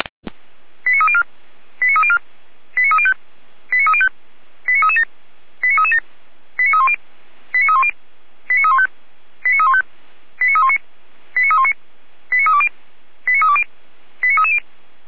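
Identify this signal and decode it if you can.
Five-tone selective call (ZVEI), heard as quick runs of five tones, tones of about 70 ms. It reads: 89304 89304 89305 89305 89337 89337 89310 89310 89315 89315 89319 89319 89320 89320 89330
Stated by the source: ZVEI five-tone selective call received by SDR.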